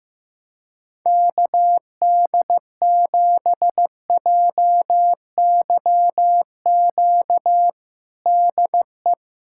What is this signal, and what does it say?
Morse code, heard as a keyed tone, about 15 wpm, 700 Hz. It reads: KD7JYQ DE
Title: USA Callsign DE Callsign 15wpm